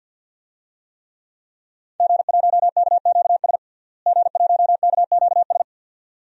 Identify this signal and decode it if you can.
Morse code, heard as a keyed tone, about 53 wpm, 700 Hz. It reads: G1KQS G1KQS
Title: DXCC-100 Call Signs - 5 Characters - Repeated 2x - 50wpm